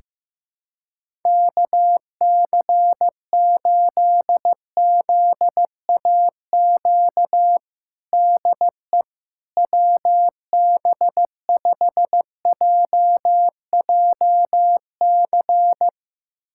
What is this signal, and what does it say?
Morse code, heard as a keyed tone, about 16 wpm, 700 Hz. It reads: KC8ZAQ DE WB5JJC
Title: USA Callsign DE Callsign 15wpm